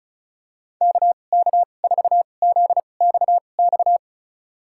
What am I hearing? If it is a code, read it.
Morse code, heard as a keyed tone, about 36 wpm, spaced as for 25 wpm, 700 Hz. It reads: KK4ZXX
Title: USA Call Signs - 6 Characters - 2x Character Spacing - 35wpm